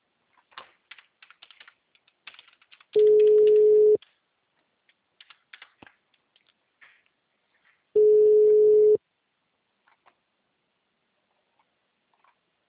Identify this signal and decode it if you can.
Ringback tone (Europe, 425 Hz).